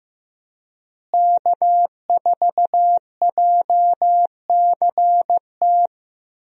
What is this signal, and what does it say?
Morse code, heard as a keyed tone, about 15 wpm, 700 Hz. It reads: K4JCT
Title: USA Call Signs - 5 Characters - Repeated 3x - 15wpm